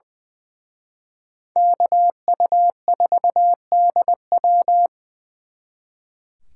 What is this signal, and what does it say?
Morse code, heard as a keyed tone, about 25 wpm, 700 Hz. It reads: KU4DW
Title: Speed Racing - USA 2x2 Call Signs 20wpm